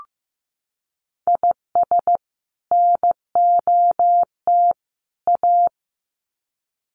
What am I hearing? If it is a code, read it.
Morse code, heard as a keyed tone, about 15 wpm, 700 Hz. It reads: IS NOT A
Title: Sets of 3 Words 15wpm